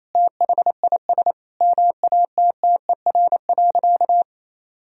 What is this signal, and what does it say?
Morse code, heard as a keyed tone, about 28 wpm, 700 Hz. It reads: THIS MATTER.
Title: Sentences from Top 1000 Words - Review Part 2 - 28wpm